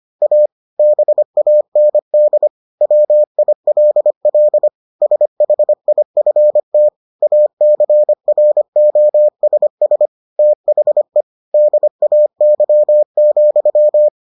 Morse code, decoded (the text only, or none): A BAND WILL SHIFT ACROSS THE DAY,